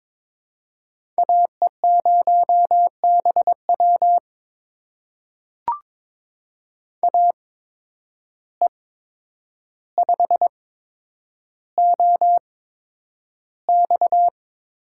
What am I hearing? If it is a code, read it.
Morse code, heard as a keyed tone, about 22 wpm, spaced as for 9 wpm, 700 Hz. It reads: AE0BW AE5OX
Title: USA Call Signs - 5 Characters - 8x Character Spacing - 22wpm